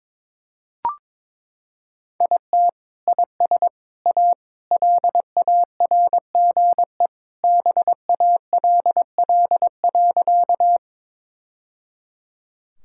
Morse code, heard as a keyed tone, about 22 wpm, 700 Hz. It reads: IT IS A LARGE BALL.
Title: Sentences from Top 1000 Words - Review Part 6 - 22wpm